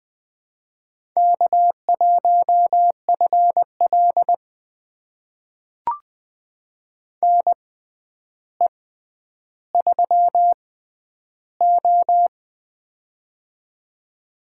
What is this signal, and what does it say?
Morse code, heard as a keyed tone, about 20 wpm, spaced as for 10 wpm, 700 Hz. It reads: K1FL NE3O
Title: USA Call Signs - 4 Characters - 6x Character Spacing - 20wpm